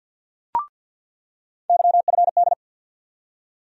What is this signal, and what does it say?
Morse code, heard as a keyed tone, about 49 wpm, 700 Hz. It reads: YFD
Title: USA Call Sign Postfixes - 3 Letters 50wpm